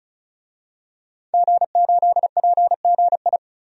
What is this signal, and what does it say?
Morse code, heard as a keyed tone, about 34 wpm, 700 Hz. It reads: G8PGI
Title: DXCC-100 Call Signs - 5 Characters - 35wpm